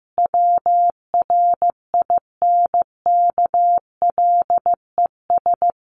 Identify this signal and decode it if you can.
Morse code, heard as a keyed tone, about 15 wpm, 700 Hz. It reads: WRINKLES
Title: A Sound Of Thunder 15wpm